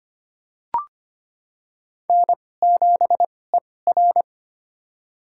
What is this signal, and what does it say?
Morse code, heard as a keyed tone, about 25 wpm, spaced as for 18 wpm, 700 Hz. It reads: N7ER
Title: USA Call Signs - 4 Characters - 2x Character Spacing - 25wpm